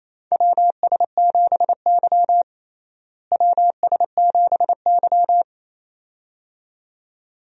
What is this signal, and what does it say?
Morse code, heard as a keyed tone, about 28 wpm, 700 Hz. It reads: WS7Y WS7Y
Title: USA Call Signs - 4 Characters - Repeated 2x - 28wpm